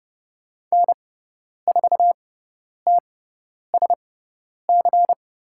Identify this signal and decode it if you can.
Morse code, heard as a keyed tone, about 30 wpm, spaced as for 10 wpm, 700 Hz. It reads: N4TSC
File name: Lesson 18g - Letter m - Call Signs 30wpm